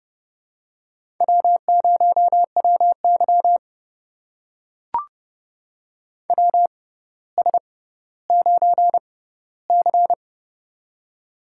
Morse code, heard as a keyed tone, about 32 wpm, spaced as for 16 wpm, 700 Hz. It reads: W0WY WS9C
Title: USA Call Signs - 4 Characters - 6x Character Spacing - 30wpm